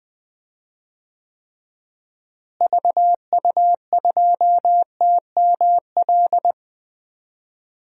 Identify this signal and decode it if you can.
Morse code, heard as a keyed tone, about 20 wpm, 700 Hz. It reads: VU2TML
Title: DXCC-100 Call Signs - 6 Characters - Repeated 2x - 20wpm